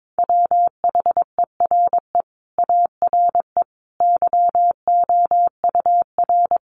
Morse code, heard as a keyed tone, about 22 wpm, 700 Hz. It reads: WHERE ARE YOUR
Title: Sentences from Top 1200 Words 22wpm